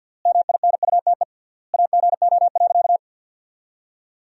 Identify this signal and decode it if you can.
Morse code, heard as a keyed tone, about 50 wpm, 700 Hz. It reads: MINUTE AGO.
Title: 3x Word-Spacing - Sentences from Top 500 Words - 50wpm